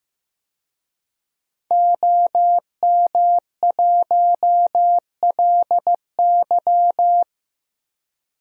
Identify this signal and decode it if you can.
Morse code, heard as a keyed tone, about 15 wpm, 700 Hz. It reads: OM1LY
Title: DXCC-100 Call Signs - 5 Characters - Repeated 3x - 15wpm